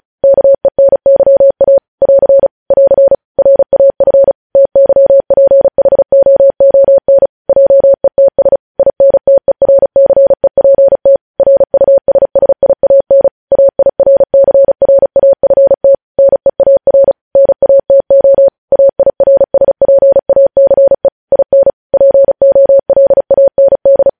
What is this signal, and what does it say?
Morse code, heard as a keyed tone, about 36 wpm, 550 Hz. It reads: KENYA <AR> <AR> RAF TYPHOON JETS INTERCEPT RUSSIAN AIRCRAFT NEAR NATO AIRSPACE IN POLAND
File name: podcast / News headlines-1x-35x35